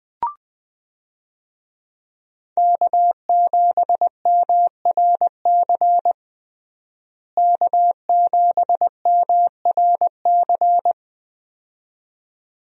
Morse code, heard as a keyed tone, about 20 wpm, 700 Hz. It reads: K7MRC K7MRC
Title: USA Call Signs - 5 Characters - Repeated 2x - 20wpm